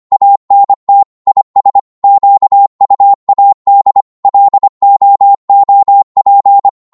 Morse code, heard as a keyed tone, about 25 wpm, 800 Hz. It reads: ANT IS QUAD LOOP
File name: Morse / qso-25wpm